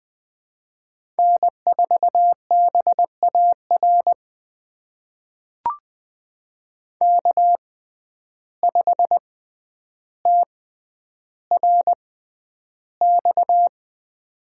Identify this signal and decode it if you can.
Morse code, heard as a keyed tone, about 20 wpm, spaced as for 10 wpm, 700 Hz. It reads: N4BAR K5TRX
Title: USA Call Signs - 5 Characters - 6x Character Spacing - 20wpm